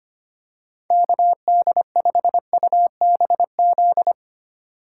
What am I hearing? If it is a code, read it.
Morse code, heard as a keyed tone, about 25 wpm, 700 Hz. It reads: KD5UBZ
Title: USA Call Signs - 3x Character Spacing - 25wpm